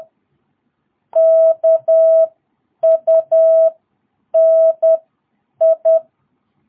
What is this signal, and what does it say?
Morse code, heard as a keyed tone, about 10 wpm, spaced as for 8 wpm, 660 Hz. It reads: KUNI
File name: jh8cbh / 2022cw